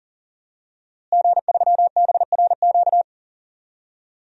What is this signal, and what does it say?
Morse code, heard as a keyed tone, about 40 wpm, 700 Hz. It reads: G3BRQ
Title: DXCC-100 Call Signs - 5 Characters - Repeated 2x - 40wpm